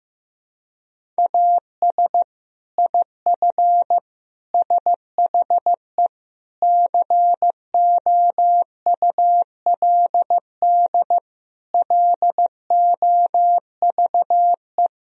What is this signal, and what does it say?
Morse code, heard as a keyed tone, about 15 wpm, 700 Hz. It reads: AS IF SHE COULD LOVE
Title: Sentences from Top 1000 Words - Review Part 1 - 15wpm